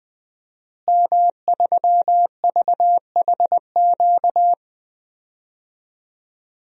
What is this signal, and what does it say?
Morse code, heard as a keyed tone, about 20 wpm, 700 Hz. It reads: M3VHQ
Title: DXCC-100 Call Signs - 5 Characters - Repeated 2x - 20wpm